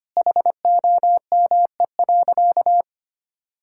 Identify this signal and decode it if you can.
Morse code, heard as a keyed tone, about 25 wpm, 700 Hz. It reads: HOME.
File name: practice / 25wpm / Sentences with 2-Word Phrases - 25wpm